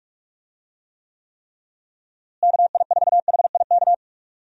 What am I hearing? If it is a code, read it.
Morse code, heard as a keyed tone, about 46 wpm, 700 Hz. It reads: KI4HIK
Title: USA Call Signs - 6 Characters - Repeated 3x - 45wpm